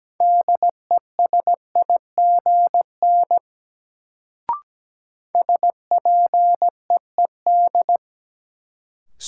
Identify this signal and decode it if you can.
Morse code, heard as a keyed tone, about 17 wpm, 700 Hz. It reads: DESIGN SPEED
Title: Sentences from Top 800 Words - New Word Drill - 17wpm